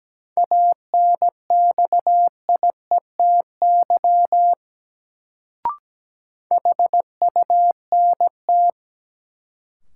Morse code, heard as a keyed tone, about 17 wpm, 700 Hz. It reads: ANXIETY HUNT
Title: Sentences from Top 1600 Words - New Word Drill - 17wpm